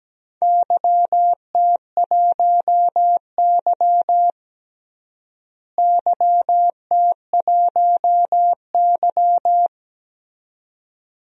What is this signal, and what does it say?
Morse code, heard as a keyed tone, about 17 wpm, 700 Hz. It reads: YT1Y YT1Y
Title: DXCC-100 Call Signs - 4 Characters - Repeated 2x - 17wpm